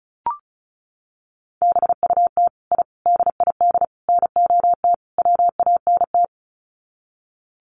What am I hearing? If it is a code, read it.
Morse code, heard as a keyed tone, about 34 wpm, 700 Hz. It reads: BUT I DID NOT WANT
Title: Sets of 5 Words Encore 35wpm